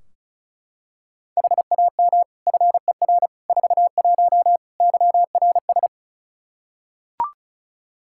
Morse code, heard as a keyed tone, about 34 wpm, 700 Hz. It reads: HAM FER 41 YRS
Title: K1USN SST QSO Practice 35wpm